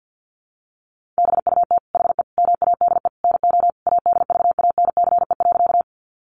Morse code, heard as a keyed tone, about 50 wpm, 700 Hz. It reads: BUT HE MADE NO ADVANCE.
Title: Sentences from Top 1400 Words 50wpm